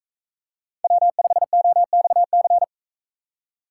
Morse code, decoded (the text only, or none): W5OXC